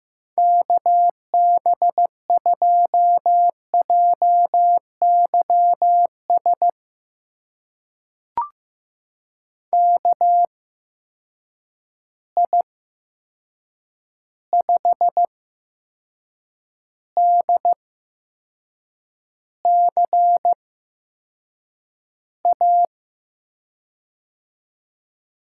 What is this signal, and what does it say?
Morse code, heard as a keyed tone, about 15 wpm, spaced as for 6 wpm, 700 Hz. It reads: KB2JYS KI5DCA
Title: USA Call Signs - 6 Characters - 8x Character Spacing - 15wpm